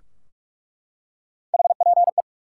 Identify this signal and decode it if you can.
Morse code, heard as a keyed tone, about 47 wpm, 700 Hz. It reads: HPE
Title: Top 90 QSO Elements 45wpm